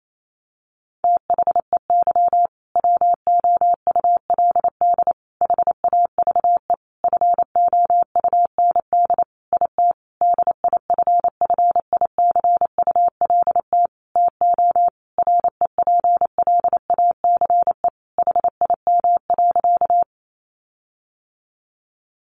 Morse code, heard as a keyed tone, about 28 wpm, 700 Hz. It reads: THEY WOULD HAVE FOUND IT DIFFICULT TO REPLACE HIM.
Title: Sentences from Top 1400 Words 28wpm